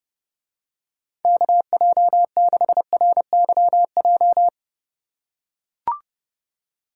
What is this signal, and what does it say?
Morse code, heard as a keyed tone, about 30 wpm, 700 Hz. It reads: KJ6RYJ